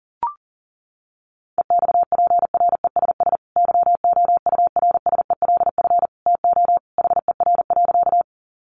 Morse code, heard as a keyed tone, about 40 wpm, 700 Hz. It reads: EXPRESS YOURSELF TO HER.
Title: Sentences from Top 1100 Words 40wpm